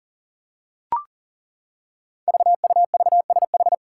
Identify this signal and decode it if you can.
Morse code, heard as a keyed tone, about 42 wpm, 700 Hz. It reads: VUVSH
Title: Five Challenging Letter-Number Combinations 40wpm